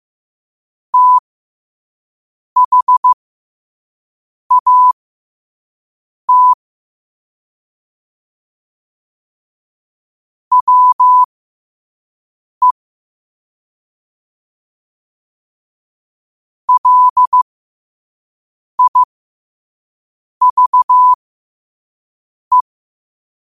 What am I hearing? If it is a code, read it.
Morse code, heard as a keyed tone, about 15 wpm, spaced as for 5 wpm, 1000 Hz. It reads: THAT WE LIVE
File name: qotdfiles / 5wpm